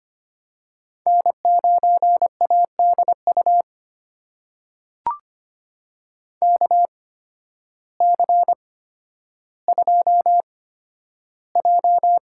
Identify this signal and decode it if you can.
Morse code, heard as a keyed tone, about 25 wpm, spaced as for 12 wpm, 700 Hz. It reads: N9ADU KC2J
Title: USA Call Signs - 8x Character Spacing - 25wpm